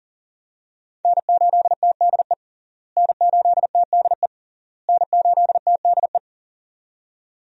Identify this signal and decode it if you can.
Morse code, heard as a keyed tone, about 40 wpm, 700 Hz. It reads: N8TDE N8TDE N8TDE
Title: USA Call Signs - 5 Characters - Repeated 3x - 40wpm